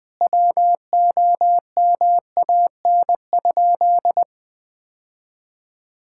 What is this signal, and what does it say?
Morse code, heard as a keyed tone, about 20 wpm, 700 Hz. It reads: WOMAN?
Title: Sentences from Top 1600 Words 20wpm